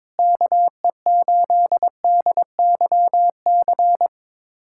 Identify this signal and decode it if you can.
Morse code, heard as a keyed tone, about 22 wpm, 700 Hz. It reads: KE8DYC
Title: USA Callsign DE Callsign - 6 Characters - 22wpm